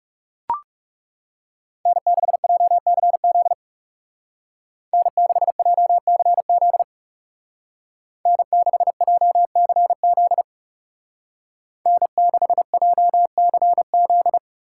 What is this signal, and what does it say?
Morse code, heard as a keyed tone, about 36 wpm, 700 Hz. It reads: N6JCZ N6JCZ N6JCZ N6JCZ
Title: Speed Racing - USA 1x3 Call Signs 30wpm